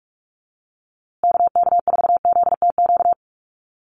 Morse code, heard as a keyed tone, about 44 wpm, 700 Hz. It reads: KK4ZTQ
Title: USA Call Signs Repeated 3x 45wpm